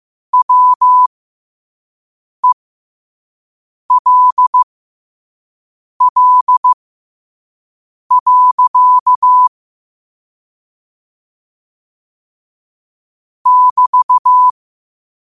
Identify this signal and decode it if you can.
Morse code, heard as a keyed tone, about 15 wpm, spaced as for 5 wpm, 1000 Hz. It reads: WELL. <BT>
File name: qotdfiles / 5wpm